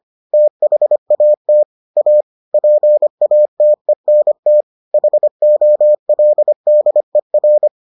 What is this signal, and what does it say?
Morse code, heard as a keyed tone, about 25 wpm, 600 Hz. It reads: THAT A PATENT HOLDER